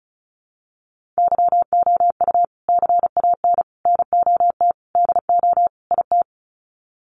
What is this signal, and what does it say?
Morse code, heard as a keyed tone, about 34 wpm, 700 Hz. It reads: YOU CAN NOT DO IT
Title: Sets of 5 Words Encore2 35wpm